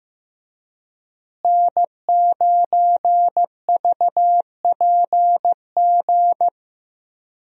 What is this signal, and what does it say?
Morse code, heard as a keyed tone, about 15 wpm, 700 Hz. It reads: N9VPG
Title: USA Call Signs - 5 Characters - Repeated 2x - 15wpm